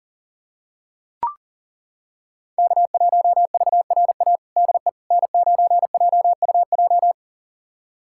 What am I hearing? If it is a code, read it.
Morse code, heard as a keyed tone, about 40 wpm, 700 Hz. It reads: K1VRA DE N9JUJ
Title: USA Callsign DE Callsign - 5 Characters - 40wpm